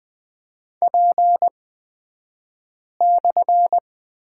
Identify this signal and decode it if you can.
Morse code, heard as a keyed tone, about 20 wpm, spaced as for 5 wpm, 700 Hz. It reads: P/
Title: 254 Lesson 41d - Period - Two Characters 20wpm